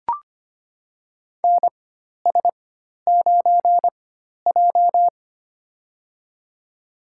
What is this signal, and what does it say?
Morse code, heard as a keyed tone, about 25 wpm, spaced as for 12 wpm, 700 Hz. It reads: NS9J
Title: USA Call Signs - 4 Characters - 4x Character Spacing - 25wpm